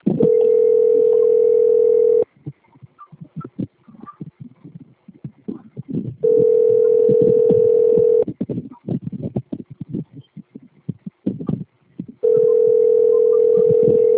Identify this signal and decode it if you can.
Ringback tone (North America), 440 + 480 Hz.